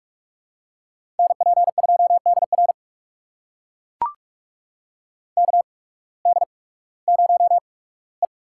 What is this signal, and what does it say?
Morse code, heard as a keyed tone, about 44 wpm, spaced as for 21 wpm, 700 Hz. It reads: NP2DR KD0E
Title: USA Call Signs - 5 Characters - 8x Character Spacing - 45wpm